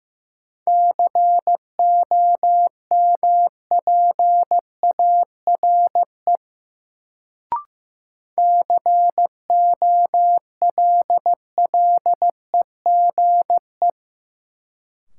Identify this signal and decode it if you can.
Morse code, heard as a keyed tone, about 15 wpm, 700 Hz. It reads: COMPARE COLLEGE
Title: Sentences from Top 1300 Words - New Word Drill - 15wpm